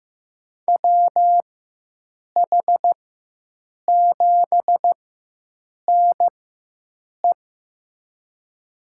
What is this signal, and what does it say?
Morse code, heard as a keyed tone, about 15 wpm, spaced as for 7 wpm, 700 Hz. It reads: WH7NE
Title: USA Call Signs - 5 Characters - 4x Character Spacing - 15wpm